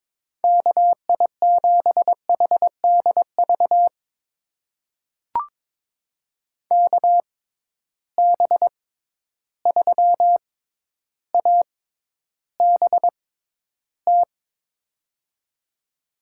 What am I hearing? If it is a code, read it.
Morse code, heard as a keyed tone, about 22 wpm, spaced as for 11 wpm, 700 Hz. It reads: KI7HDV KB3ABT
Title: USA Call Signs - 6 Characters - 6x Character Spacing - 22wpm